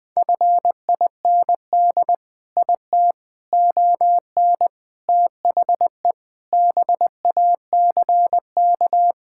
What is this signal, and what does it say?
Morse code, heard as a keyed tone, about 20 wpm, 700 Hz. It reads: FIND IT ON THE BACK
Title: Sentences with 4-Word Phrases - 20wpm